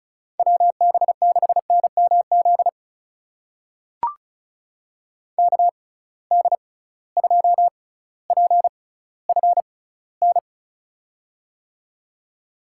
Morse code, heard as a keyed tone, about 36 wpm, spaced as for 18 wpm, 700 Hz. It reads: WB6NMZ KD2PFN
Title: USA Call Signs - 6 Characters - 6x Character Spacing - 35wpm